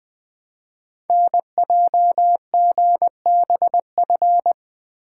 Morse code, heard as a keyed tone, about 20 wpm, 700 Hz. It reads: NJGBF
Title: Code Groups - 5 Letters - 20wpm